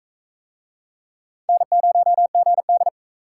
Morse code, heard as a keyed tone, about 40 wpm, 700 Hz. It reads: N0GD